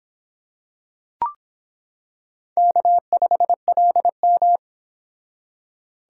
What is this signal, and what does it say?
Morse code, heard as a keyed tone, about 26 wpm, 700 Hz. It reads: K5LM